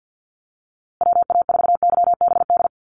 Morse code, heard as a keyed tone, about 50 wpm, 700 Hz. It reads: WA4YBD
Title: USA Call Signs Repeated 3x 50wpm